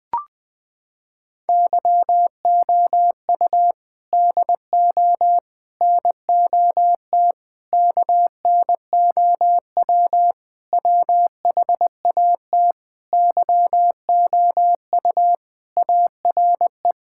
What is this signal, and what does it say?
Morse code, heard as a keyed tone, about 20 wpm, 700 Hz. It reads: YOU DO NOT KNOW WHAT YOU ARE
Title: Sentences with 5-Word Phrases - 20wpm